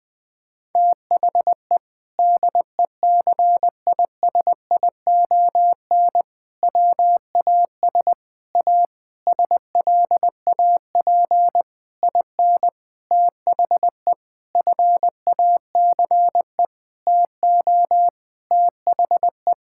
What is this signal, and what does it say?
Morse code, heard as a keyed tone, about 21 wpm, 700 Hz. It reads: THE DECISION WAS A SLAP IN THE FACE TO THE
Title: Sentences with 6-Word Phrases - 20wpm